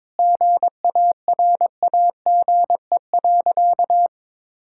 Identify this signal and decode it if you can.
Morse code, heard as a keyed tone, about 22 wpm, 700 Hz. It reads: GARAGE.
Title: Sentences from Top 1600 Words 22wpm